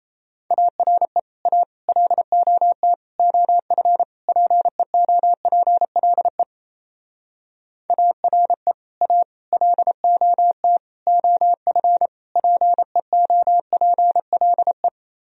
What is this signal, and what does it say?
Morse code, heard as a keyed tone, about 36 wpm, 700 Hz. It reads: ARE A LOT OF PEOPLE ARE A LOT OF PEOPLE